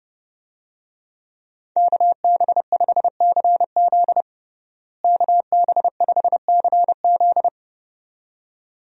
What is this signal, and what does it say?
Morse code, heard as a keyed tone, about 32 wpm, 700 Hz. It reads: KB5CZ KB5CZ
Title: USA Call Signs - 5 Characters - Repeated 2x - 30wpm